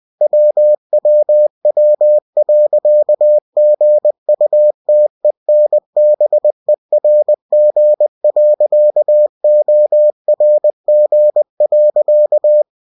Morse code, heard as a keyed tone, about 20 wpm, 600 Hz. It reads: WWW.GUTENBERG.ORG.
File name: winnie-20wpm